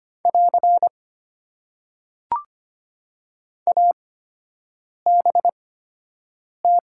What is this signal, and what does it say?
Morse code, heard as a keyed tone, about 25 wpm, spaced as for 7 wpm, 700 Hz. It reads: <AR> ABT
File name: Top 90 QSO Elements - 8x Character Spacing - 25wpm